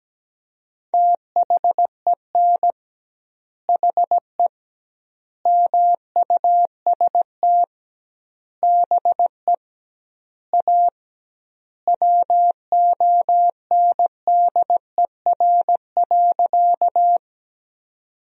Morse code, heard as a keyed tone, about 17 wpm, 700 Hz. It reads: THEN HE MUST BE A WONDER.
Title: 2x Word-Spacing - Sentences from Top 500 Words - 17wpm